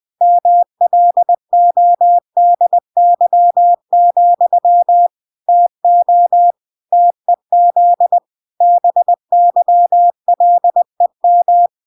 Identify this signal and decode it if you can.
Morse code, heard as a keyed tone, about 20 wpm, 700 Hz. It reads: MLODY, TO TEZ BYLEM